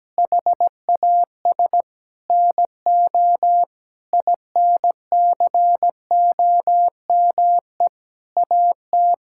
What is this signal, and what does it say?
Morse code, heard as a keyed tone, about 17 wpm, 700 Hz. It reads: HAS NO INCOME AT
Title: Sentences from Top 1000 Words - Review Part 5 - 17wpm